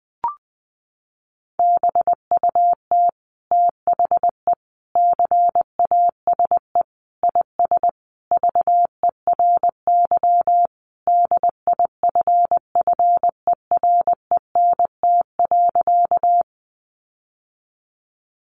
Morse code, heard as a keyed tone, about 19 wpm, 700 Hz. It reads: BUT THE CASE IS VERY DIFFERENT.
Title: Sentences from Top 700 Words 20wpm